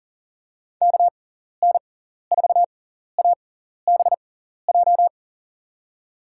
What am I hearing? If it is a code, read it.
Morse code, heard as a keyed tone, about 40 wpm, spaced as for 14 wpm, 700 Hz. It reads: KN4ABJ
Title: USA Call Signs - 6 Characters - 6x Character Spacing - 40wpm